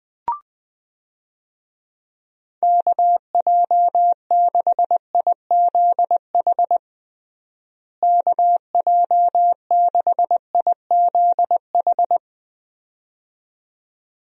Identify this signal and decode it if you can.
Morse code, heard as a keyed tone, about 21 wpm, 700 Hz. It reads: KJ6IZH KJ6IZH
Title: USA Call Signs - 6 Characters - Repeated 2x - 20wpm